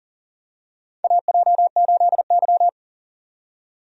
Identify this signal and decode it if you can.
Morse code, heard as a keyed tone, about 40 wpm, 700 Hz. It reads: AJ8Y
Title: USA Call Signs - 4 Characters - 40wpm